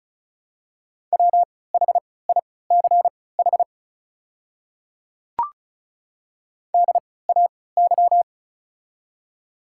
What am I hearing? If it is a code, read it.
Morse code, heard as a keyed tone, about 36 wpm, spaced as for 20 wpm, 690 Hz. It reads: WHICH DAY
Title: 3x Character Spacing - Top 100 Words - 35wpm